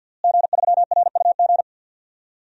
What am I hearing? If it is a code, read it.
Morse code, heard as a keyed tone, about 53 wpm, 700 Hz. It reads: G3RUG
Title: DXCC-100 Call Signs - 5 Characters - Repeated 2x - 50wpm